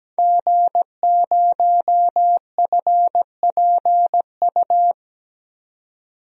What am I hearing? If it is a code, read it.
Morse code, heard as a keyed tone, about 17 wpm, 700 Hz. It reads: G0FPU
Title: DXCC-100 Call Signs - 5 Characters - 17wpm